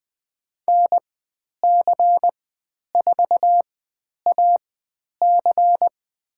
Morse code, heard as a keyed tone, about 20 wpm, spaced as for 10 wpm, 700 Hz. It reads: NC4AC
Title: 109 Lesson 18g - Letter m - Call Signs 20wpm